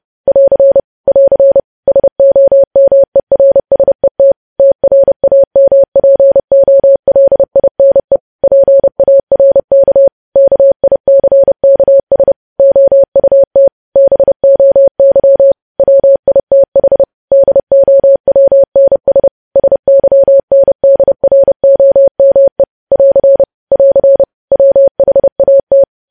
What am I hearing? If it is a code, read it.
Morse code, heard as a keyed tone, about 30 wpm, 550 Hz. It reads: <AR> <AR> SOMERSET TRAMPOLINE PARK KICKS OUT BOY WITH DOWNS SYNDROME <AR> <AR> WHAT
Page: news headlines in morse code at 30 wpm